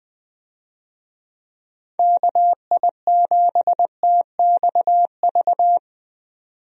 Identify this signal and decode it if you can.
Morse code, heard as a keyed tone, about 20 wpm, 700 Hz. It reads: KI7TXV
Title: USA Call Signs - 6 Characters - 20wpm